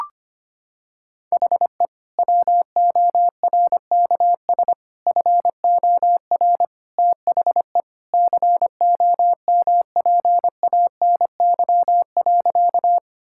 Morse code, heard as a keyed tone, about 25 wpm, 700 Hz. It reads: HE WORKS FOR THE COMPANY.
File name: practice / 25wpm / Sentences with 2-Word Phrases - 25wpm